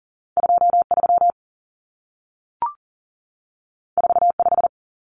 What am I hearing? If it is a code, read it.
Morse code, heard as a keyed tone, about 40 wpm, 700 Hz. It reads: 23 45
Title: Numbers - 2-Digit 40wpm